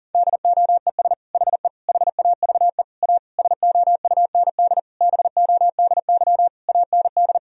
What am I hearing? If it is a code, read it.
Morse code, heard as a keyed tone, about 40 wpm, 700 Hz. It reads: DOES HE HAVE A SOUND BODY AND
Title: Sentences from Top 1000 Words - Review Part 2 - 40wpm